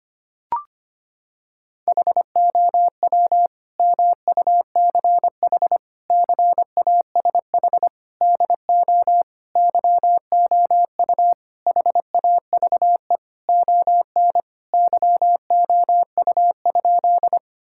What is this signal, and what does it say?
Morse code, heard as a keyed tone, about 25 wpm, 700 Hz. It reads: HOW MUCH CASH DO YOU HAVE ON YOU?
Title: Sentences from Top 1000 Words - Review Part 5 - 25wpm